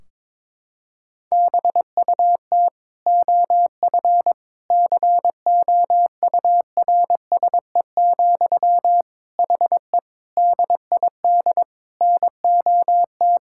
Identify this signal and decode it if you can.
Morse code, heard as a keyed tone, about 22 wpm, 700 Hz. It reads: BUT OF COURSE, HE DID NOT
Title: Sentences from Top 1600 Words 22wpm